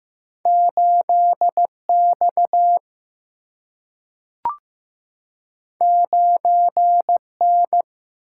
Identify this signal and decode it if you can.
Morse code, heard as a keyed tone, about 15 wpm, 700 Hz. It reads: 8 X 9 N
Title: Single-Letter Single-Number 15wpm